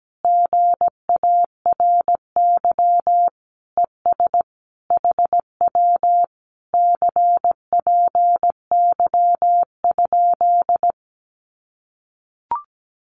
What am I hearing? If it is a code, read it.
Morse code, heard as a keyed tone, about 17 wpm, 700 Hz. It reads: GARY ES HW CPY?
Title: Common QSO Phrases Encore 17wpm